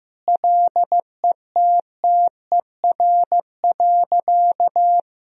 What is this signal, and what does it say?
Morse code, heard as a keyed tone, about 15 wpm, 700 Hz. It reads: LETTER.
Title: Sentences from Top 1000 Words - Review Part 4 - 15wpm